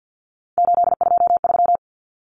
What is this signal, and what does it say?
Morse code, heard as a keyed tone, about 50 wpm, 700 Hz. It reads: 823
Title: Common Cut Numbers - 3-Digit 50wpm